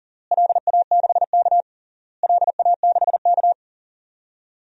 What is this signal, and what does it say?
Morse code, heard as a keyed tone, about 40 wpm, 700 Hz. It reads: LA6K LA6K
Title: DXCC-100 Call Signs - 4 Characters - Repeated 3x - 40wpm